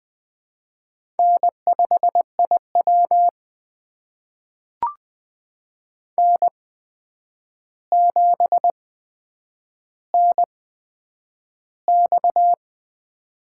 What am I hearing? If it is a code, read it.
Morse code, heard as a keyed tone, about 20 wpm, spaced as for 9 wpm, 700 Hz. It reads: N5IW N7NX
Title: USA Call Signs - 4 Characters - 8x Character Spacing - 20wpm